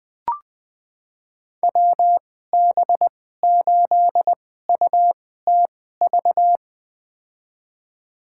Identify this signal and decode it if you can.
Morse code, heard as a keyed tone, about 21 wpm, spaced as for 15 wpm, 700 Hz. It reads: WB8UTV
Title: USA Call Signs - 6 Characters - 2x Character Spacing - 20wpm